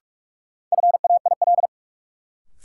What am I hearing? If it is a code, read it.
Morse code, heard as a keyed tone, about 47 wpm, 700 Hz. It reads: FAIL
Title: Sentences from Top 1100 Words - New Word Drill - 45wpm